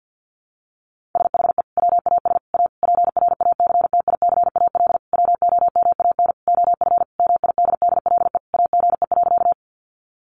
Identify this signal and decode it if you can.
Morse code, heard as a keyed tone, about 50 wpm, 700 Hz. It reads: SHE WAS A PRACTICAL WOMAN OF MIDDLE AGE.
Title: Sentences from Top 1400 Words 50wpm